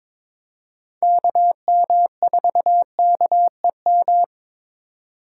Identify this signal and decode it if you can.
Morse code, heard as a keyed tone, about 22 wpm, 700 Hz. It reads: KM4KEM
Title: USA Call Signs - 8x Character Spacing - 22wpm